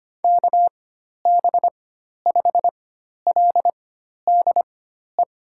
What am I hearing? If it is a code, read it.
Morse code, heard as a keyed tone, about 25 wpm, spaced as for 12 wpm, 700 Hz. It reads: KB5LDE
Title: USA Call Signs - 6 Characters - 4x Character Spacing - 25wpm